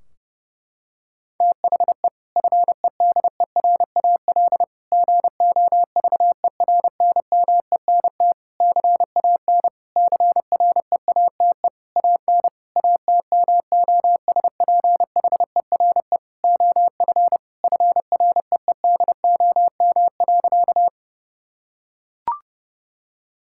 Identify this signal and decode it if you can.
Morse code, heard as a keyed tone, about 30 wpm, 700 Hz. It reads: THE FEDERAL GOVERNMENT CAN CREATE AN ATMOSPHERE OF FREEDOM.
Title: Sentences from Top 1600 Words 30wpm